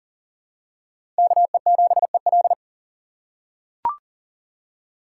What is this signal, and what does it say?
Morse code, heard as a keyed tone, about 40 wpm, 700 Hz. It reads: KE7EL
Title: USA Call Signs - 5 Characters - 2x Character Spacing - 40wpm